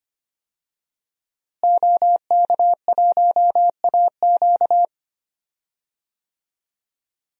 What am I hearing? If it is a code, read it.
Morse code, heard as a keyed tone, about 25 wpm, 700 Hz. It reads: OK1AQ
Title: DXCC-100 Call Signs - 5 Characters - Repeated 2x - 25wpm